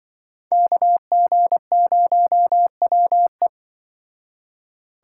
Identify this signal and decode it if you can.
Morse code, heard as a keyed tone, about 24 wpm, 700 Hz. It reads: KG0WE
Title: Speed Racing - USA 2x2 Call Signs 20wpm